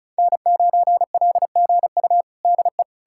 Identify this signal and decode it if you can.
Morse code, heard as a keyed tone, about 34 wpm, 700 Hz. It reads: N9LGU DE
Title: USA Callsign DE Callsign - 5 Characters - 35wpm